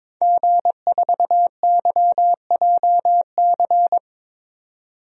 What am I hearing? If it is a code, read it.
Morse code, heard as a keyed tone, about 22 wpm, 700 Hz. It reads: G4YJC
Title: DXCC-100 Call Signs - 5 Characters - 22wpm